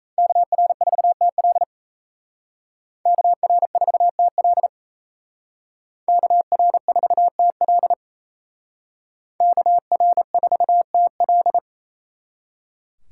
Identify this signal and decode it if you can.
Morse code, heard as a keyed tone, about 37 wpm, 700 Hz. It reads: KR4TL KR4TL KR4TL KR4TL